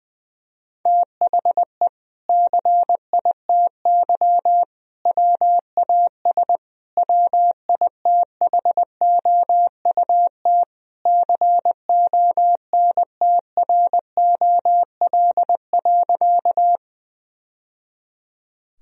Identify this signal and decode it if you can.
Morse code, heard as a keyed tone, about 20 wpm, 700 Hz. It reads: THE CITY WAS WITHOUT CONTROL.
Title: Sentences from Top 1000 Words - Review Part 3 - 20wpm